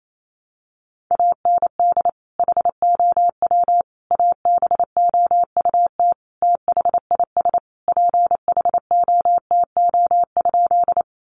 Morse code, heard as a keyed tone, about 28 wpm, 700 Hz. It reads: AND HOW ABOUT THIS PHOTO?
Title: Sentences from Top 1200 Words 28wpm